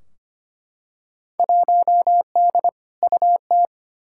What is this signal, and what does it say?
Morse code, heard as a keyed tone, about 25 wpm, 700 Hz. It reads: 1D UT
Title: Field Day Exchanges - 25wpm